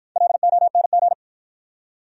Morse code, heard as a keyed tone, about 51 wpm, 700 Hz. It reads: LONG